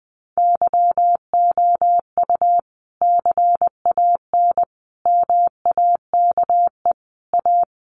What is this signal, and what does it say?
Morse code, heard as a keyed tone, about 20 wpm, 700 Hz. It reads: YOU CAN MAKE A
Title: Sentences from Top 1400 Words 20wpm